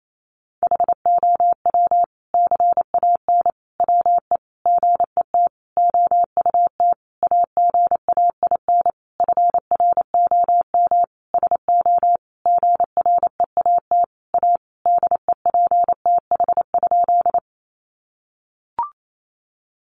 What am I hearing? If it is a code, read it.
Morse code, heard as a keyed tone, about 28 wpm, 700 Hz. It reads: HOW CAN WE GET OUT AGAIN FROM SO GREAT A DEPTH?
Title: Sentences from Top 1200 Words 28wpm